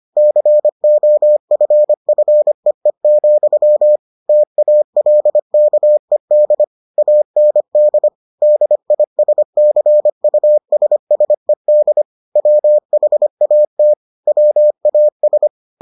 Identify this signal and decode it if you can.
Morse code, heard as a keyed tone, about 25 wpm, 600 Hz. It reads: COFFEE, TALKED AND DISCUSSED WHAT WAS